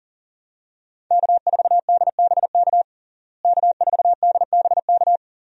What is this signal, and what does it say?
Morse code, heard as a keyed tone, about 40 wpm, 700 Hz. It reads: K4DBK K4DBK
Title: USA Call Signs - 5 Characters - Repeated 2x - 40wpm